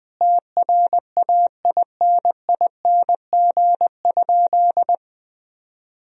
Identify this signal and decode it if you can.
Morse code, heard as a keyed tone, about 20 wpm, 700 Hz. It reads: TRAINING?
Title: Sentences from Top 1000 Words - Review Part 4 - 20wpm